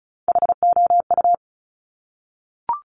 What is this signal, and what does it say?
Morse code, heard as a keyed tone, about 36 wpm, 700 Hz. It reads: HOU
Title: Top Three Letter Combinations in Words 35wpm